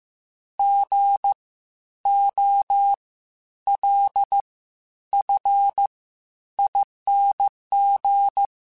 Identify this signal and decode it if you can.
Morse code, heard as a keyed tone, about 15 wpm, spaced as for 10 wpm, 800 Hz. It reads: GOLFING
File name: Spaced Alphas / Spaced Alphas mp3s / Gs sp @13